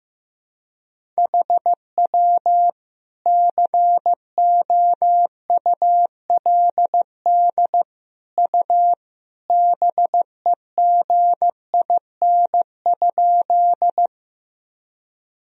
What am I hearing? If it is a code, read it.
Morse code, heard as a keyed tone, about 15 wpm, 700 Hz. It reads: HW COULD U BEGIN?
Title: Sentences from Top 300 Words and Abbreviations 15wpm